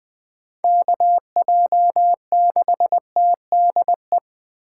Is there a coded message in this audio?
Yes: Morse code from 0.6 to 4.2 s, heard as a keyed tone, about 19 wpm, 700 Hz. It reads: KJ6TDE